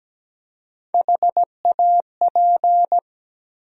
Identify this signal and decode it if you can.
Morse code, heard as a keyed tone, about 17 wpm, 700 Hz. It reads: HAP
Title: USA Call Sign Postfixes - 3 Letters 17wpm